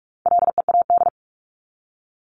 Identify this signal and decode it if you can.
Morse code, heard as a keyed tone, about 44 wpm, 700 Hz. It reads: LEAD
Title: ING Suffix - Top 100 Words with Base Words 45wpm